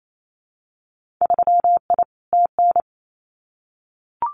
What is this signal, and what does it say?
Morse code, heard as a keyed tone, about 28 wpm, 700 Hz. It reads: 3I TN
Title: Winter Field Day Exchanges 28wpm